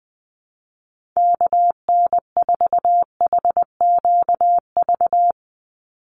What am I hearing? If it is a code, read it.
Morse code, heard as a keyed tone, about 20 wpm, 700 Hz. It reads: KN4HQV
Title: USA Call Signs 20wpm